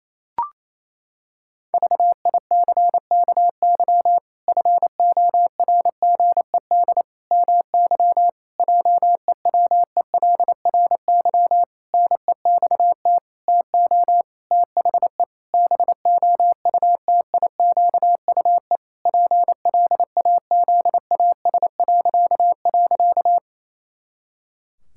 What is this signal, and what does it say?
Morse code, heard as a keyed tone, about 28 wpm, 700 Hz. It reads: VICKY FORGED MY JEWELRY NEXT TO THE BOUTIQUE PLAZAS..
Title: Pangram Challenge v2 - 28wpm